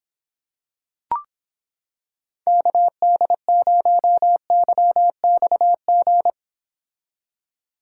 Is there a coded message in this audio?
Yes: Morse code from 2.5 to 6.3 s, heard as a keyed tone, about 26 wpm, 700 Hz. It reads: KD0YXG